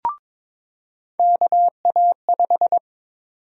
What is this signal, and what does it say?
Morse code, heard as a keyed tone, about 22 wpm, 700 Hz. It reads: KA5
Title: USA Call Sign Prefixes - 22wpm